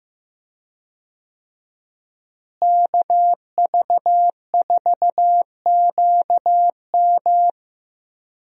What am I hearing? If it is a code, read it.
Morse code, heard as a keyed tone, about 15 wpm, 700 Hz. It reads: KV4QM
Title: USA Call Signs - 5 Characters - Repeated 2x - 15wpm